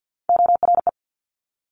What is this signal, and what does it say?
Morse code, heard as a keyed tone, about 50 wpm, 700 Hz. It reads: ORE
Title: Multi-letter Phonemes Quick Fire 50wpm